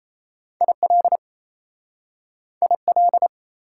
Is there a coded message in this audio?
Yes: Morse code from 0.6 to 3.3 s, heard as a keyed tone, about 37 wpm, 690 Hz. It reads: IL IL